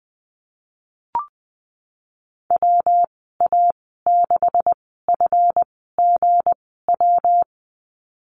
Morse code, heard as a keyed tone, about 20 wpm, spaced as for 15 wpm, 700 Hz. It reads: WA6FGW
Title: 167 Lesson 27g - Number 7 - Call Signs 20wpm